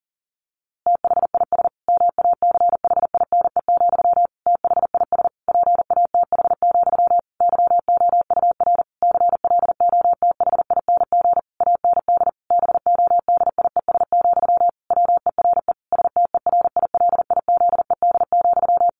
Morse code, heard as a keyed tone, about 40 wpm, 700 Hz. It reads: THIS MACHINE, THIS PATH, YOUR CLOTHING AND BODIES, WERE STERILIZED,
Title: A Sound Of Thunder 40wpm